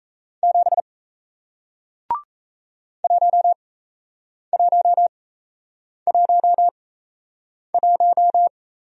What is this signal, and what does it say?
Morse code, heard as a keyed tone, about 38 wpm, spaced as for 8 wpm, 700 Hz. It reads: 7 1111